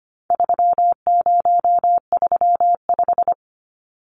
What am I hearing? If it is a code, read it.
Morse code, heard as a keyed tone, about 25 wpm, 700 Hz. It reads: 3035
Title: Numbers - 4-Digit 25wpm